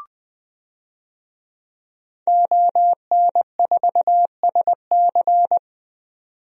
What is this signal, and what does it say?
Morse code, heard as a keyed tone, about 20 wpm, 700 Hz. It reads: ON4SC
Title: DXCC-100 Call Signs - 5 Characters - 20wpm